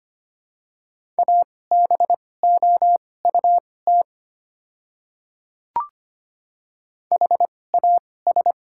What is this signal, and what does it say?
Morse code, heard as a keyed tone, about 25 wpm, spaced as for 18 wpm, 700 Hz. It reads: ABOUT HAS
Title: 2x Character Spacing - Top 100 Words - 25wpm